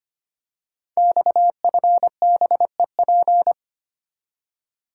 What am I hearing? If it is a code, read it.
Morse code, heard as a keyed tone, about 25 wpm, 700 Hz. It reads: XFBEP
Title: Code Groups - 5 Letter:Numbers - 25wpm